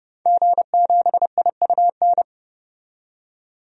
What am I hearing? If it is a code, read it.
Morse code, heard as a keyed tone, about 30 wpm, 700 Hz. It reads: G7IUN
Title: DXCC-100 Call Signs - 5 Characters - 30wpm